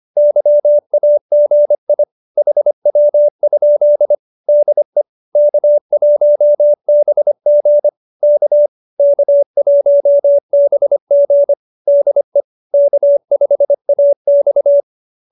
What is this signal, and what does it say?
Morse code, heard as a keyed tone, about 25 wpm, 590 Hz. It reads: YAGI HW? DE K1BG K K1BG DE K5AX
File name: qso201b 25